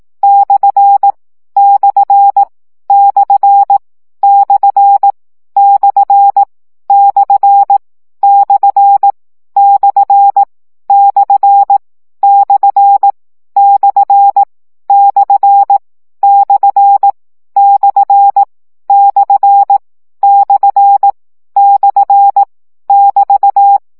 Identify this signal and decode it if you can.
Morse code, heard as a keyed tone, about 18 wpm, 800 Hz. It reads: / / / / / / / / / / / / / / / / / <BT>